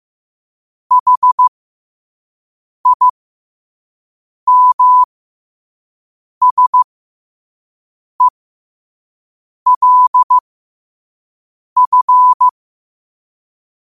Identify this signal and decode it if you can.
Morse code, heard as a keyed tone, about 15 wpm, spaced as for 5 wpm, 1000 Hz. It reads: HIMSELF
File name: qotdfiles / 5wpm